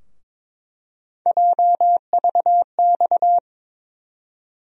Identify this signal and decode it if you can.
Morse code, heard as a keyed tone, about 22 wpm, 700 Hz. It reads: JVX
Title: USA Call Sign Postfixes - 3 Letters 22wpm